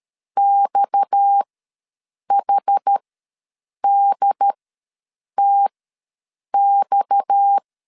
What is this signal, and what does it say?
Morse code, heard as a keyed tone, about 13 wpm, spaced as for 7 wpm, 800 Hz. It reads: XHDTX
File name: LEZIONI / 6 PKX0 / 6 PKX ws 12wpm 3106